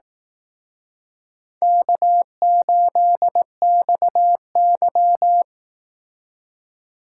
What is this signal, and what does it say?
Morse code, heard as a keyed tone, about 18 wpm, 700 Hz. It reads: K8XY
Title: Speed Racing - USA 1x2 Call Signs 15wpm